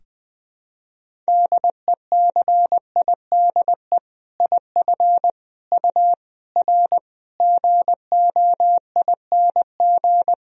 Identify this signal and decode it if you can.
Morse code, heard as a keyed tone, about 20 wpm, 700 Hz. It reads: DECIDE IF U R GOING
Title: Sentences from Top 500 Words and Abbreviations 20wpm